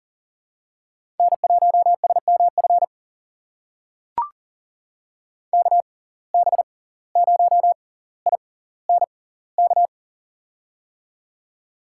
Morse code, heard as a keyed tone, about 38 wpm, spaced as for 19 wpm, 700 Hz. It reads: N1SMF KB0INK
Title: USA Call Signs - 6x Character Spacing - 40wpm